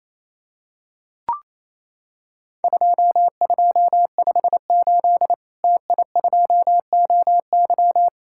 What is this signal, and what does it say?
Morse code, heard as a keyed tone, about 28 wpm, 700 Hz. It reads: 2258 TI2OY
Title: Worked All Europe DX Contest - QTC Exchange - 28wpm